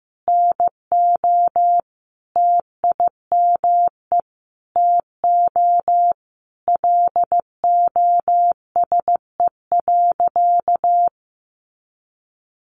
Morse code, heard as a keyed tone, about 15 wpm, 700 Hz. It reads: NO TIME TO LOSE.
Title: Sentences from Top 1100 Words 15wpm